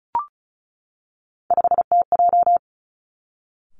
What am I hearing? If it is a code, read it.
Morse code, heard as a keyed tone, about 36 wpm, 700 Hz. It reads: 5TJ
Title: Three Challenging Letter-Number Combinations 35wpm